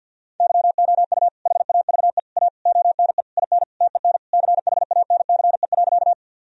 Morse code, heard as a keyed tone, about 52 wpm, 700 Hz. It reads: YOU HAVE A ONE IN TEN CHANCE.
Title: Sentences from Top 1000 Words - Review Part 2 - 50wpm